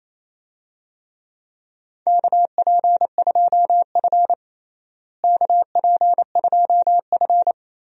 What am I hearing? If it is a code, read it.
Morse code, heard as a keyed tone, about 28 wpm, 700 Hz. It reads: KP2F KP2F
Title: DXCC-100 Call Signs - 4 Characters - Repeated 3x - 28wpm